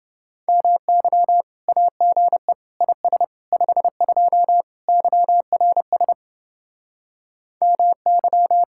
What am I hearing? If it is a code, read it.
Morse code, heard as a keyed tone, about 29 wpm, 700 Hz. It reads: MY AGE IS 52 YRS MY